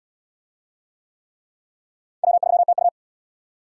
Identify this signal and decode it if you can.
Morse code, heard as a keyed tone, about 68 wpm, 700 Hz. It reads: W3TU